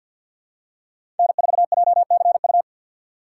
Morse code, heard as a keyed tone, about 52 wpm, 700 Hz. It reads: N4JCU
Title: USA Call Signs - 5 Characters - 6x Character Spacing - 50wpm